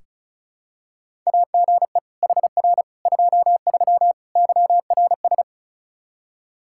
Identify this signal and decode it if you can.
Morse code, heard as a keyed tone, about 36 wpm, 700 Hz. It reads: AGE HR 23 YRS
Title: Practice QSO - Exchange 3 - More Info - 35wpm